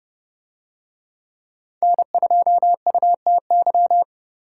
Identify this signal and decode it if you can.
Morse code, heard as a keyed tone, about 30 wpm, 700 Hz. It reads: N2UTY